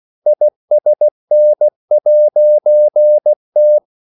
Morse code, heard as a keyed tone, about 16 wpm, 600 Hz. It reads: ISN'T
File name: CW / Practice / EnglishQuotations0116